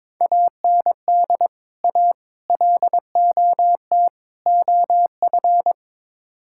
Morse code, heard as a keyed tone, about 22 wpm, 700 Hz. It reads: AND A LOT OF
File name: practice / 22wpm / Build A Phrase - 5-Word Phrases - 22wpm